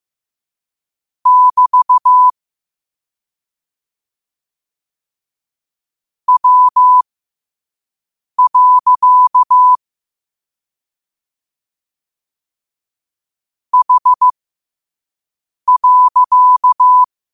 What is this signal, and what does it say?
Morse code, heard as a keyed tone, about 15 wpm, spaced as for 5 wpm, 1000 Hz. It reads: <BT> W. H.
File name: qotdfiles / 5wpm